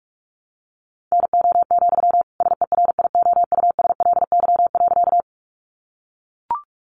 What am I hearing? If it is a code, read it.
Morse code, heard as a keyed tone, about 44 wpm, 700 Hz. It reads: NO, SERIOUSLY.
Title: Sentences from Top 1200 Words 45wpm